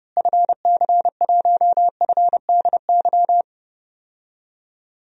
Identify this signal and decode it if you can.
Morse code, heard as a keyed tone, about 30 wpm, 700 Hz. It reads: FC1FDY
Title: DXCC-100 Call Signs - 6 Characters - Repeated 3x - 30wpm